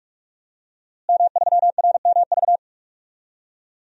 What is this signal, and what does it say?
Morse code, heard as a keyed tone, about 44 wpm, 700 Hz. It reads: M3RMV
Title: DXCC-100 Call Signs - 5 Characters - Repeated 2x - 45wpm